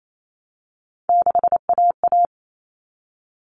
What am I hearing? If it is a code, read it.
Morse code, heard as a keyed tone, about 28 wpm, 700 Hz. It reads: 6AA
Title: All Cut Numbers - 3-Digit 28wpm